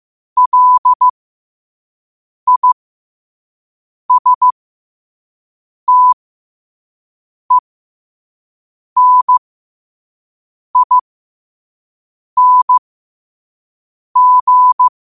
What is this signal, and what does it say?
Morse code, heard as a keyed tone, about 15 wpm, spaced as for 5 wpm, 1000 Hz. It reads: LISTENING
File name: qotdfiles / 5wpm